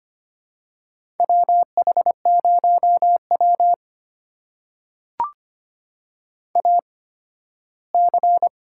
Morse code, heard as a keyed tone, about 25 wpm, spaced as for 15 wpm, 700 Hz. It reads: WH0W AC
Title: USA Call Signs - 4 Characters - 8x Character Spacing - 25wpm